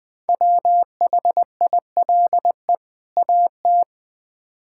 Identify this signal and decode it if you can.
Morse code, heard as a keyed tone, about 20 wpm, 700 Hz. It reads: WHILE AT
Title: Build A Phrase - 5-Word Phrases - 20wpm